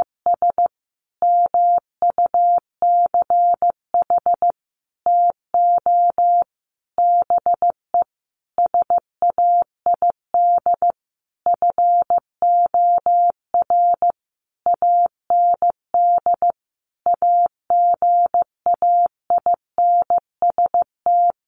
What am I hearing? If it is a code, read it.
Morse code, heard as a keyed tone, about 14 wpm, 700 Hz. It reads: S MUCH TO BE SAID FOR AND AGAINST
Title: Sentences from Top 1300 Words 15wpm